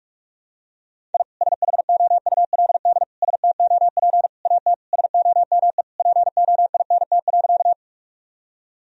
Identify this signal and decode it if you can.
Morse code, heard as a keyed tone, about 46 wpm, 700 Hz. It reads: I SHOULD STOP AT SOME POINT.
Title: Sentences from Top 1000 Words - Review Part 1 - 45wpm